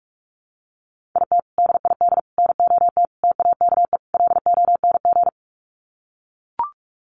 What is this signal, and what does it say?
Morse code, heard as a keyed tone, about 44 wpm, 700 Hz. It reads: IT DID NOT TAKE LONG
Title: Sets of 5 Words Encore2 45wpm